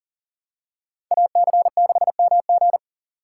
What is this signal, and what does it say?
Morse code, heard as a keyed tone, about 40 wpm, 700 Hz. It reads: AC6MG
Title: USA Call Signs - 5 Characters - 3x Character Spacing - 40wpm